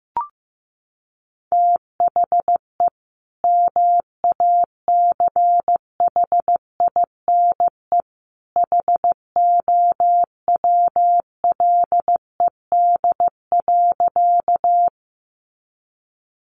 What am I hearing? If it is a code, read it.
Morse code, heard as a keyed tone, about 15 wpm, 700 Hz. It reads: THE MACHINE HOWLED.
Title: A Sound Of Thunder 15wpm